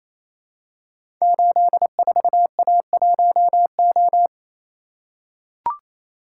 Morse code, heard as a keyed tone, about 29 wpm, 700 Hz. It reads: 84A1O
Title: Code Groups - 5 Letter:Numbers - 28wpm